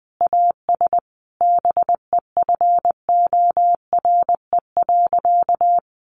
Morse code, heard as a keyed tone, about 20 wpm, 700 Hz. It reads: AS BEFORE.
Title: Sentences from Top 1400 Words 20wpm